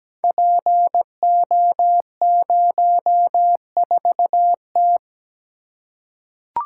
Morse code, heard as a keyed tone, about 17 wpm, 700 Hz. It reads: PO04T
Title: Code Groups - 5 Letter:Numbers - 17wpm